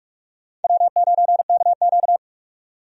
Morse code, heard as a keyed tone, about 46 wpm, 700 Hz. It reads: W9KQ
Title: USA Call Signs - 4 Characters - 45wpm